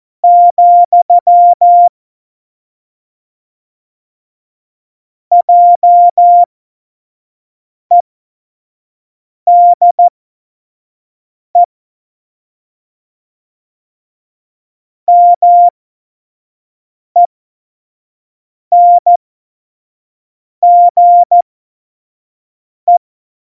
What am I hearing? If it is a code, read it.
Morse code, heard as a keyed tone, about 14 wpm, spaced as for 5 wpm, 700 Hz. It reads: , JEDE MENGE
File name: Morsekurs / Witze / Witz 21. Mai 2019, 25 F70